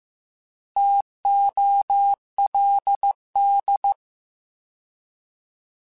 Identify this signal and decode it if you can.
Morse code, heard as a keyed tone, about 15 wpm, 800 Hz. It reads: TOLD